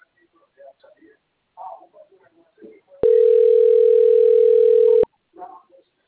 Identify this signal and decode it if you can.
Ringback tone (North America), 440 + 480 Hz.